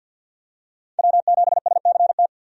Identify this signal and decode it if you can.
Morse code, heard as a keyed tone, about 51 wpm, 700 Hz. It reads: W7SCT
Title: USA Call Signs - 5 Characters - 50wpm